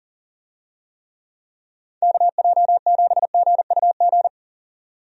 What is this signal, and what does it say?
Morse code, heard as a keyed tone, about 40 wpm, 700 Hz. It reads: KJ7GUG
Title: USA Call Signs - 6 Characters - 40wpm